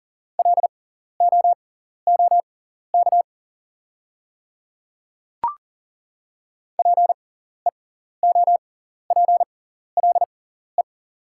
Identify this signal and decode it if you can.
Morse code, heard as a keyed tone, about 37 wpm, spaced as for 14 wpm, 700 Hz. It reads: LOOK PEOPLE
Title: 6x Character Spacing - Top 100 Words - 40wpm